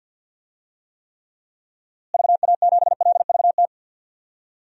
Morse code, heard as a keyed tone, about 51 wpm, 700 Hz. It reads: VA7LVT
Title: DXCC-100 Call Signs - 6 Characters - Repeated 3x - 50wpm